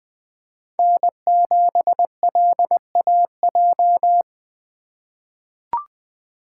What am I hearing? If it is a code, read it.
Morse code, heard as a keyed tone, about 21 wpm, 700 Hz. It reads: N7LAJ
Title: USA Call Signs - 5 Characters - 4x Character Spacing - 20wpm